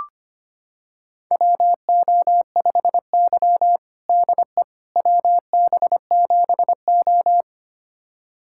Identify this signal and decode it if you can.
Morse code, heard as a keyed tone, about 25 wpm, 700 Hz. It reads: WO5Y DE WB7O
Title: USA Callsign DE Callsign - 4 Characters - 25wpm